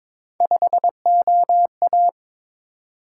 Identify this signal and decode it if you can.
Morse code, heard as a keyed tone, about 22 wpm, 700 Hz. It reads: 5OA
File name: practice / 22wpm / Three Letter-Number Combinations - 22wpm